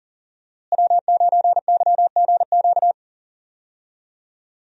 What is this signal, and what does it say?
Morse code, heard as a keyed tone, about 39 wpm, 700 Hz. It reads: W9YGQ
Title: USA Call Signs - 5 Characters - Repeated 3x - 40wpm